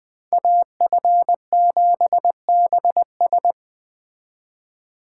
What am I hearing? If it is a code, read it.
Morse code, heard as a keyed tone, about 20 wpm, 700 Hz. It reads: AF7BS
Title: Speed Racing - USA Call Signs 17wpm